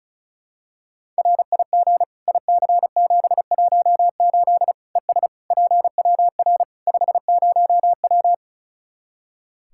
Morse code, heard as a keyed tone, about 34 wpm, 700 Hz. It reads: RIG IC718 ES PWR 50W
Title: Practice QSO - Exchange 2 - Info - 35wpm